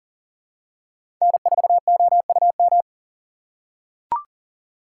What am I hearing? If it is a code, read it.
Morse code, heard as a keyed tone, about 40 wpm, 700 Hz. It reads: N4OUM
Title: USA Call Signs - 3x Character Spacing - 40wpm